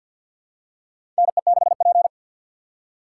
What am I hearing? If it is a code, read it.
Morse code, heard as a keyed tone, about 46 wpm, 700 Hz. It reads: NE6P
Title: USA Call Signs - 4 Characters - 3x Character Spacing - 50wpm